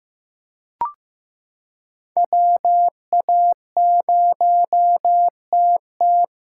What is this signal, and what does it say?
Morse code, heard as a keyed tone, about 16 wpm, 700 Hz. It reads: WA0TT
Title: USA Callsign DE Callsign - 5 Characters - 15wpm